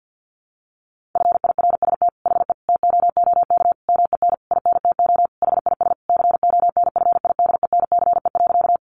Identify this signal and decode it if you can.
Morse code, heard as a keyed tone, about 48 wpm, 700 Hz. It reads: FIRST HE TOOK MEN INTO HIS CONFIDENCE.
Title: Sentences from Top 1500 Words 50wpm